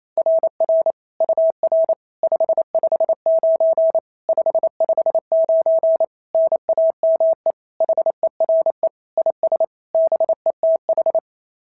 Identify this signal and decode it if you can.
Morse code, heard as a keyed tone, about 27 wpm, 690 Hz. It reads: RR UR 559 559 NAME HERE IS BETH
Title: SOTA QSO Practice 28wpm